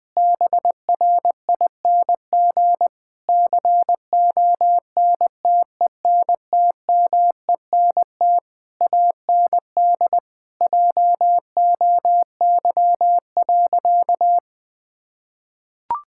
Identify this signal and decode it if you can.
Morse code, heard as a keyed tone, about 20 wpm, 700 Hz. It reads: BRING CONTENTMENT AND JOY.
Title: Difficult Sentences with Hints - 20wpm